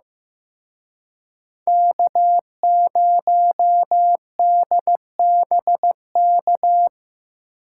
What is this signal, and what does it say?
Morse code, heard as a keyed tone, about 18 wpm, 700 Hz. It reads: K0DBK
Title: Speed Racing - USA 1x3 Call Signs 15wpm